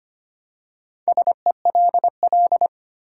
Morse code, heard as a keyed tone, about 25 wpm, 700 Hz. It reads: SELL
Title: Four Letter Words - 25wpm